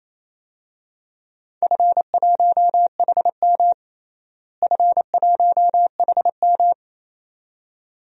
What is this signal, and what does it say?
Morse code, heard as a keyed tone, about 28 wpm, 700 Hz. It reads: F1HM F1HM
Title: DXCC-100 Call Signs - 4 Characters - Repeated 2x - 28wpm